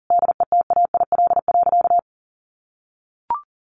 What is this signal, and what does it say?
Morse code, heard as a keyed tone, about 40 wpm, 700 Hz. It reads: DETAIL.
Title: Sentences from Top 1500 Words 40wpm